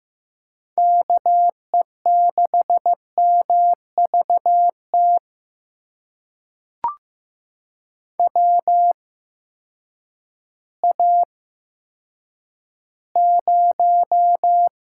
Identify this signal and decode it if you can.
Morse code, heard as a keyed tone, about 15 wpm, spaced as for 9 wpm, 700 Hz. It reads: KE6MVT WA0
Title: USA Call Signs - 6 Characters - 8x Character Spacing - 15wpm